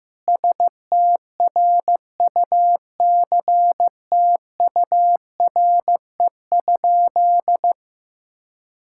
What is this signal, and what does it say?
Morse code, heard as a keyed tone, about 16 wpm, 700 Hz. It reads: STRUCTURE?
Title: Sentences from Top 1000 Words - Review Part 6 - 15wpm